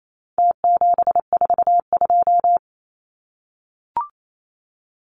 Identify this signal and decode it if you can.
Morse code, heard as a keyed tone, about 28 wpm, 700 Hz. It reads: T742
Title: Numbers - Contest Style - 4-Digit 28wpm